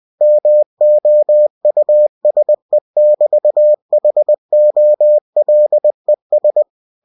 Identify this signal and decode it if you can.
Morse code, heard as a keyed tone, about 20 wpm, 600 Hz. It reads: MOUSE<BT>HOLES
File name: cw-books / winnie-the-pooh / winnie-20wpm-page-0066